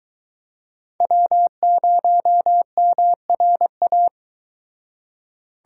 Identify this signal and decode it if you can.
Morse code, heard as a keyed tone, about 23 wpm, 700 Hz. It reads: W0MRA